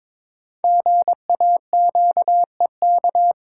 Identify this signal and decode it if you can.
Morse code, heard as a keyed tone, about 22 wpm, 700 Hz. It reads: GAQEK
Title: Code Groups - 5 Letters - 22wpm